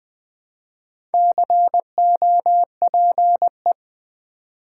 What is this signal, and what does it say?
Morse code, heard as a keyed tone, about 20 wpm, 700 Hz. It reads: COPE
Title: Build A Word - 2 Letters At A Time - 4-Letter Words - 20wpm